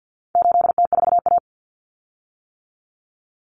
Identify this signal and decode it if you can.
Morse code, heard as a keyed tone, about 50 wpm, 700 Hz. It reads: 8T4A
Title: Common Cut Numbers - 4-Digit 50wpm